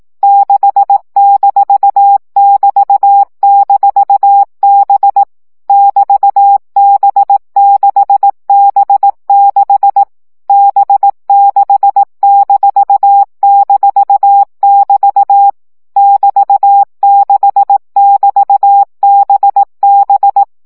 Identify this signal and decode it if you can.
Morse code, heard as a keyed tone, about 18 wpm, 800 Hz. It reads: 6-<BT>-B <BT>B6B6 B6--<BT> <BT>6<BT>BB
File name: CW gyakorlas b 6 egyenlo minusz 15 baud